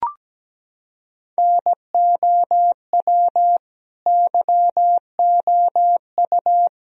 Morse code, heard as a keyed tone, about 17 wpm, 700 Hz. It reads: NOW YOU
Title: Sentences from Top 1000 Words - Review Part 2 - 17wpm